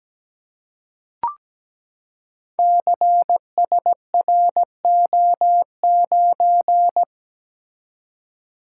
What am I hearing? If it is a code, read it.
Morse code, heard as a keyed tone, about 17 wpm, 700 Hz. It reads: CSRO9
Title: Code Groups - 5 Letter:Numbers - 17wpm